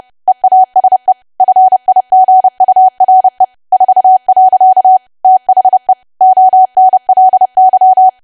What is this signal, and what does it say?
Morse code, heard as a keyed tone, about 30 wpm, 750 Hz. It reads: EASE FIGURE 4. THE ONLY
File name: cwa / W1AW-Level3 / 30WPM